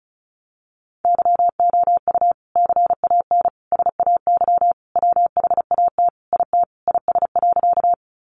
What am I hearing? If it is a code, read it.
Morse code, heard as a keyed tone, about 34 wpm, 700 Hz. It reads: YOU CAN SAY WHAT IT IS.
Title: Sentences from Top 200 Words 35wpm